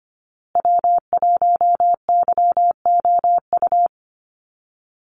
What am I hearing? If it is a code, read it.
Morse code, heard as a keyed tone, about 25 wpm, 700 Hz. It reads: W1YOU
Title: USA Call Signs Repeated 3x 25wpm